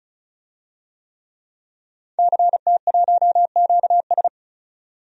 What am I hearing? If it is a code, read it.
Morse code, heard as a keyed tone, about 34 wpm, 700 Hz. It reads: CT1QS
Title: DXCC-100 Call Signs - 5 Characters - 35wpm